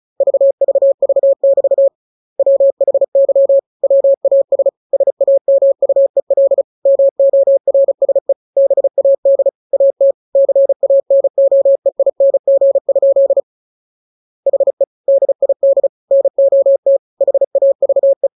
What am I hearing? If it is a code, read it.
Morse code, heard as a keyed tone, about 35 wpm, 550 Hz. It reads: VVV<BT> WHY WAS SAMUEL MORSE BAD AT CANOEING? HE DID NOT HAVE
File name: jokes / joke 1 35 0000